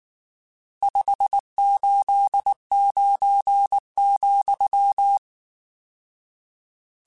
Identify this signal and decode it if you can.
Morse code, heard as a keyed tone, about 19 wpm, 800 Hz. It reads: 589,
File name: RSTs / RST-mp3s / Various RST @18